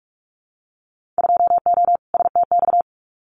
Morse code, heard as a keyed tone, about 46 wpm, 700 Hz. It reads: 2O STX
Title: Winter Field Day Exchanges 45wpm